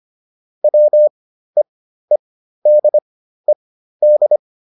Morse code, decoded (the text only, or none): WEEDED